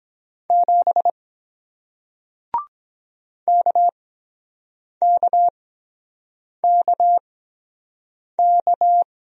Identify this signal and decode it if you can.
Morse code, heard as a keyed tone, about 23 wpm, spaced as for 7 wpm, 700 Hz. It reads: 7 KKKK